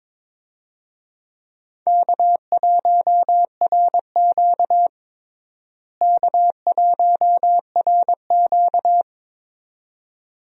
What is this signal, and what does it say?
Morse code, heard as a keyed tone, about 21 wpm, 700 Hz. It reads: K1RQ K1RQ
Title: USA Call Signs - 4 Characters - Repeated 2x - 22wpm